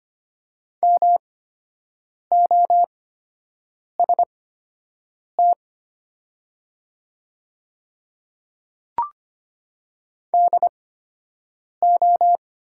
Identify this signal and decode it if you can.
Morse code, heard as a keyed tone, about 25 wpm, spaced as for 7 wpm, 700 Hz. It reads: MOST DO
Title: 8x Character Spacing - Top 200 Words - 25wpm